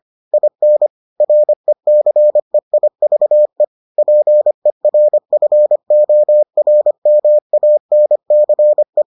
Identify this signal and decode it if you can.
Morse code, heard as a keyed tone, about 25 wpm, 600 Hz. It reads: IN RECEIVE PERFORMANCE